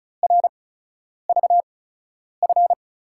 Morse code, heard as a keyed tone, about 36 wpm, spaced as for 10 wpm, 700 Hz. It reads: RVF
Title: USA Call Signs - 8x Character Spacing - 35wpm